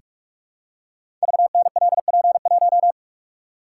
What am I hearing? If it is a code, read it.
Morse code, heard as a keyed tone, about 44 wpm, 700 Hz. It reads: VNLP1
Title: Code Groups - 5 Letter:Numbers - 45wpm